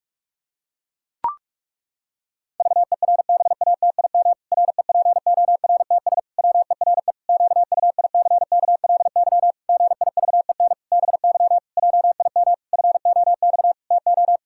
Morse code, heard as a keyed tone, about 41 wpm, 700 Hz. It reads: VERBATIM REPORTS WERE QUICKLY GIVEN BY JIM FOX TO
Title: Pangram Challenge v2 - 45wpm